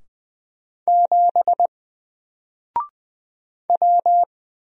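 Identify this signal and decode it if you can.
Morse code, heard as a keyed tone, about 20 wpm, 700 Hz. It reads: 7 W
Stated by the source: Single letter and number Morse drill at 20 wpm, from N4LCD.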